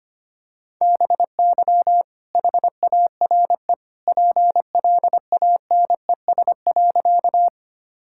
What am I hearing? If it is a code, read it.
Morse code, heard as a keyed tone, about 25 wpm, 700 Hz. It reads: BY HARE PLANES.
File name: practice / 25wpm / All-Age Appropriate Jokes - 25wpm